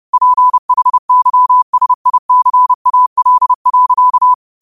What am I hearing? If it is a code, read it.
Morse code, heard as a keyed tone, about 30 wpm, 1000 Hz. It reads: PHYSICAL.
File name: qotdfiles / 30wpm